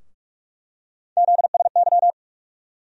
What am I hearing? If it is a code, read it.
Morse code, heard as a keyed tone, about 43 wpm, 700 Hz. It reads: ZSY
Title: USA Call Sign Postfixes - 3 Letters 45wpm